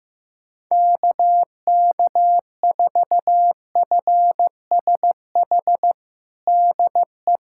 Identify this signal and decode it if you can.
Morse code, heard as a keyed tone, about 15 wpm, 700 Hz. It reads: KK4FSH DE
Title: USA Callsign DE Callsign - 6 Characters - 15wpm